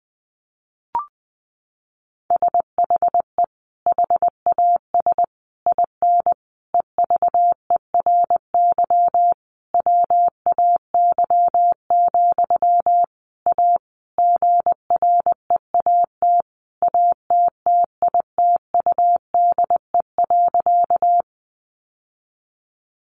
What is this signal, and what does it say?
Morse code, heard as a keyed tone, about 20 wpm, 700 Hz. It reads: SHE HAS IN EVERY WAY, A GREAT ATTITUDE.
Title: Sentences from Top 1300 Words 20wpm